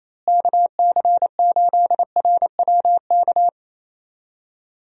Morse code, heard as a keyed tone, about 28 wpm, 700 Hz. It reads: KC8RWK
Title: USA Call Signs - 6 Characters - Repeated 2x - 28wpm